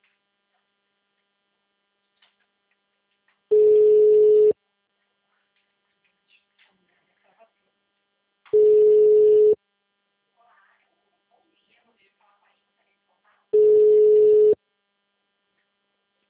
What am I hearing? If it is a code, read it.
Ringback tone (Europe, 425 Hz), with speech.
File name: out- -2014- - - .11832